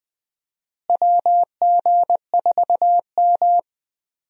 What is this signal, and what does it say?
Morse code, heard as a keyed tone, about 21 wpm, 700 Hz. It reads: WG4M
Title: USA Call Signs - 4 Characters - Repeated 2x - 20wpm